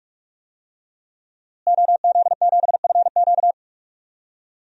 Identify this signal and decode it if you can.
Morse code, heard as a keyed tone, about 43 wpm, 700 Hz. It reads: OZ7FQ
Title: DXCC-100 Call Signs - 5 Characters - Repeated 2x - 45wpm